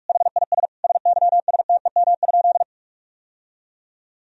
Morse code, heard as a keyed tone, about 46 wpm, 700 Hz. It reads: HIS SYSTEM?
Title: Sentences from Top 1000 Words - Review Part 3 - 45wpm